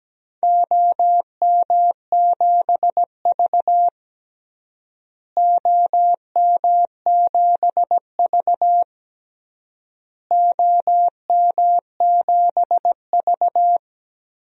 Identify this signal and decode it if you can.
Morse code, heard as a keyed tone, about 17 wpm, 700 Hz. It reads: OM7V OM7V OM7V
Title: DXCC-100 Call Signs - 4 Characters - Repeated 3x - 17wpm